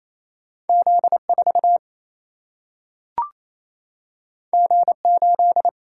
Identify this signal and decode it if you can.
Morse code, heard as a keyed tone, about 28 wpm, 700 Hz. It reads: Z 4 G 8
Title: Single-Letter Single-Number 28wpm